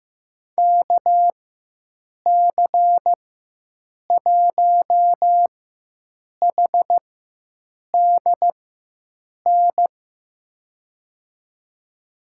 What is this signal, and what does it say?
Morse code, heard as a keyed tone, about 15 wpm, spaced as for 7 wpm, 700 Hz. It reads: KC1HDN
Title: USA Call Signs - 6 Characters - 4x Character Spacing - 15wpm